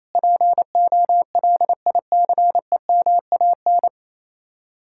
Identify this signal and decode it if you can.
Morse code, heard as a keyed tone, about 37 wpm, 700 Hz. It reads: POLICEMAN